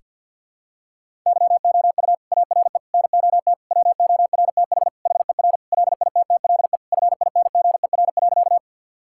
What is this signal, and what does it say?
Morse code, heard as a keyed tone, about 50 wpm, 700 Hz. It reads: YOU ARE NOT WORTH HER LITTLE FINGER.
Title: Sentences from Top 1600 Words 50wpm